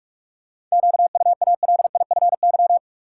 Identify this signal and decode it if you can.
Morse code, heard as a keyed tone, about 46 wpm, 700 Hz. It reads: QUALIFY
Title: Seven Letter Words - 45wpm